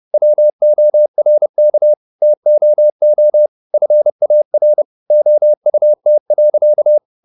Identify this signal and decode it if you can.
Morse code, heard as a keyed tone, about 30 wpm, 600 Hz. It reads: WORK TOO FAR OUT.
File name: the old man and the sea / the old man and the sea 1 30wpm